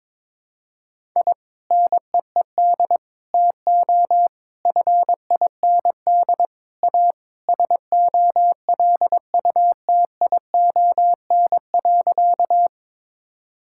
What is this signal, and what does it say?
Morse code, heard as a keyed tone, about 22 wpm, 700 Hz. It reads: I NEED TO FIND A SOLUTION.
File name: practice / 22wpm / Sentences with 3-Word Phrases - 22wpm